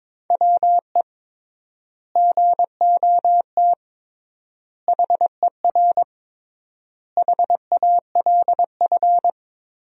Morse code, heard as a keyed tone, about 22 wpm, 700 Hz. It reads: WE GOT HER HALF
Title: 3x Word-Spacing - Sentences from Top 400 Words - 22wpm